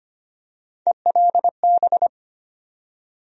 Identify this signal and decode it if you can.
Morse code, heard as a keyed tone, about 25 wpm, 700 Hz. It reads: ELB
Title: USA Call Sign Postfixes - 3 Letters 25wpm